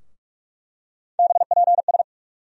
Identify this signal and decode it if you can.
Morse code, heard as a keyed tone, about 46 wpm, 700 Hz. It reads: BPS
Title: USA Call Sign Postfixes - 3 Letters 45wpm